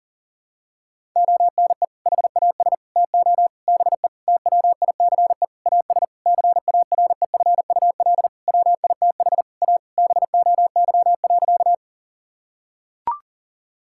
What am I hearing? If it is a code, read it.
Morse code, heard as a keyed tone, about 40 wpm, 700 Hz. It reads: ONE HAS TO BE TWICE AS CAREFUL WITH A BOY.
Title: Sentences from Top 1600 Words 40wpm